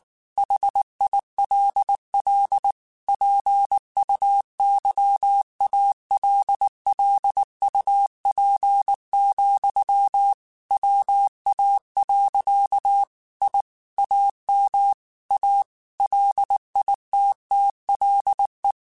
Morse code, heard as a keyed tone, about 19 wpm, 800 Hz. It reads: HILL PUYALLUP, WA. I AM A LITTLE